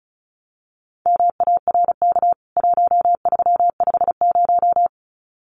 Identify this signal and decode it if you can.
Morse code, heard as a keyed tone, about 34 wpm, 700 Hz. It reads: MARK 1350
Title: CWT Exchange 35wpm